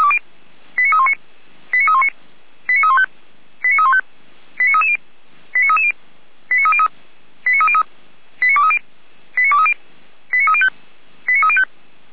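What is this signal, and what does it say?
Five-tone selective call (ZVEI), heard as quick runs of five tones, tones of about 70 ms. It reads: 89319 89319 89315 89315 89330 89330 89303 89303 89230 89230 89305 89305
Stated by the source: ZVEI five-tone selective call received by SDR.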